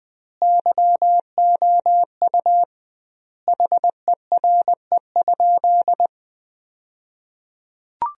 Morse code, heard as a keyed tone, about 21 wpm, 700 Hz. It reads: YOU HERE?
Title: 2x Word-Spacing - Sentences from Top 500 Words - 20wpm